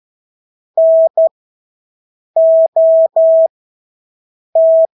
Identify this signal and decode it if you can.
Morse code, heard as a keyed tone, about 12 wpm, spaced as for 6 wpm, 650 Hz. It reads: NOT
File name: Engelske ord 12-6 wpm